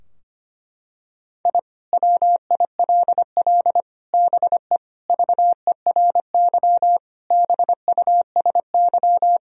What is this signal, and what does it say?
Morse code, heard as a keyed tone, about 25 wpm, 700 Hz. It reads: I WILL BE VERY BUSY
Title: Sentences from Top 1000 Words - Review Part 3 - 25wpm